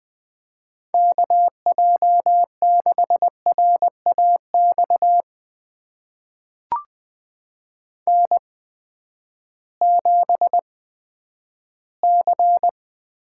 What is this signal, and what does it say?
Morse code, heard as a keyed tone, about 20 wpm, spaced as for 11 wpm, 700 Hz. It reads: KJ6RAX N7C
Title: USA Call Signs - 8x Character Spacing - 20wpm